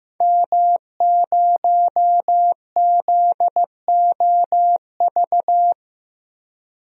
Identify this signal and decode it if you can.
Morse code, heard as a keyed tone, about 15 wpm, 700 Hz. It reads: M0ZOV
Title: DXCC-100 Call Signs - 5 Characters - 15wpm